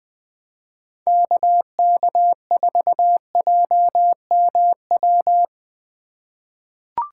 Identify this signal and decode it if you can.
Morse code, heard as a keyed tone, about 20 wpm, 700 Hz. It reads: KK4JMW
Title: USA Call Signs - 6 Characters - 2x Character Spacing - 20wpm